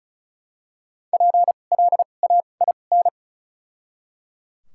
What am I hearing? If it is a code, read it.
Morse code, heard as a keyed tone, about 36 wpm, 700 Hz. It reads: P L A I N
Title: 2x Character Spacing - Top 500 Words - 35wpm